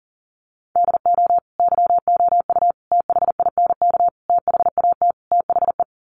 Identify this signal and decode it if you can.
Morse code, heard as a keyed tone, about 40 wpm, 700 Hz. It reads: DO YOU THINK THAT THE
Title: Sets of 5 Words Encore 40wpm